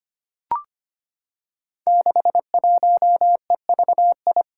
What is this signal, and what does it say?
Morse code, heard as a keyed tone, about 25 wpm, 700 Hz. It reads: 61EVI
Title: Five Challenging Letter-Number Combinations 25wpm